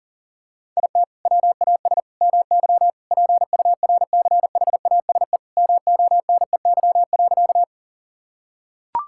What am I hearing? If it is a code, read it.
Morse code, heard as a keyed tone, about 40 wpm, 700 Hz. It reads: IT WAS MY PURCHASE MONEY.
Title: Sentences from Top 1000 Words - Review Part 5 - 40wpm